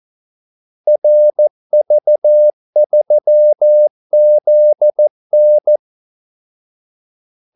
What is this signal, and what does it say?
Morse code, heard as a keyed tone, about 14 wpm, 600 Hz. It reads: RV3ZN
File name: Callsigns-14wpm